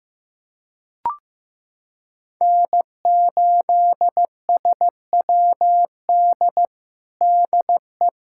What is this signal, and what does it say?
Morse code, heard as a keyed tone, about 15 wpm, 700 Hz. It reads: N8SWD DE
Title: USA Callsign DE Callsign 15wpm